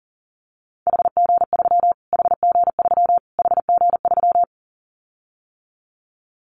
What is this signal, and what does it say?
Morse code, heard as a keyed tone, about 40 wpm, 700 Hz. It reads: HG3 HG3 HG3
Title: Callsign Prefixes for Top 25 DXCC - Repeated 3x - 40wpm